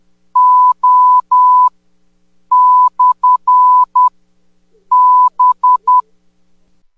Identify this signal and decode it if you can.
Morse code, heard as a keyed tone, about 18 wpm, spaced as for 8 wpm, 1000 Hz. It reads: O/B
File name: 10m-beacon